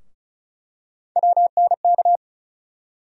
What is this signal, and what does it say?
Morse code, heard as a keyed tone, about 35 wpm, 700 Hz. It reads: WNK
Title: USA Call Sign Postfixes - 3 Letters 35wpm